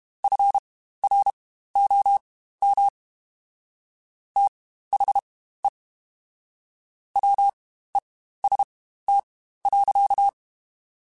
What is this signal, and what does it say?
Morse code, heard as a keyed tone, about 32 wpm, spaced as for 15 wpm, 800 Hz. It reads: FROM THE WEST.
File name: Spaced Characters / Spaced Characters-mp3s / Weather5 1 sp @30